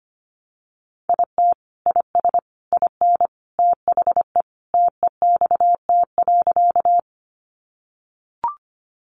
Morse code, heard as a keyed tone, about 25 wpm, 700 Hz. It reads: IT IS IN THE TEXT.
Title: Sentences from Top 1200 Words 25wpm